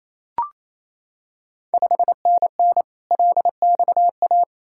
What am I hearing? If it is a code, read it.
Morse code, heard as a keyed tone, about 27 wpm, 690 Hz. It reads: 5NN LXA
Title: Ontario QSO Party Exchange 28wpm